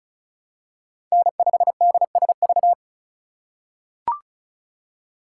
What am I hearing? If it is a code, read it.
Morse code, heard as a keyed tone, about 36 wpm, 700 Hz. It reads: N5DSV
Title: USA Call Signs - 5 Characters - 2x Character Spacing - 35wpm